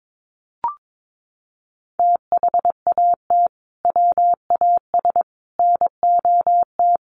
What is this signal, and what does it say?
Morse code, heard as a keyed tone, about 22 wpm, 700 Hz. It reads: THAT WAS NOT
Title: Sentences from Top 1400 Words 22wpm